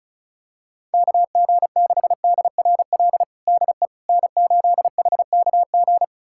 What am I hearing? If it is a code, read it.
Morse code, heard as a keyed tone, about 34 wpm, 700 Hz. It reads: KG6DRL DE N8HKG
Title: USA Callsign DE Callsign 35wpm